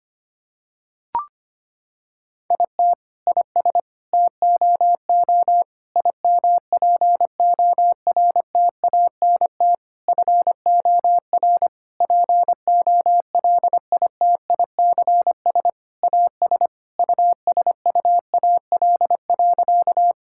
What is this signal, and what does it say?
Morse code, heard as a keyed tone, about 25 wpm, 700 Hz. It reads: IT IS TOO IMPORTANT FOR POLITICS AS USUAL.
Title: Sentences from Top 1000 Words - Review Part 6 - 25wpm